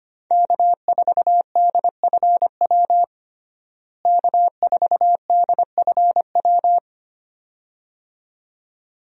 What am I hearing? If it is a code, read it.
Morse code, heard as a keyed tone, about 25 wpm, 700 Hz. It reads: K4DFW K4DFW
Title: USA Call Signs - 5 Characters - Repeated 2x - 25wpm